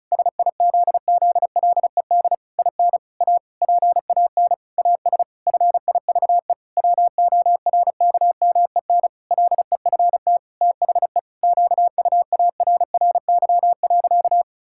Morse code, heard as a keyed tone, about 36 wpm, 700 Hz. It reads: SIZZLED IN A PAN AS FIVE WORKMEN LEFT THE QUARRY.
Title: Pangram Challenge v2 - 35wpm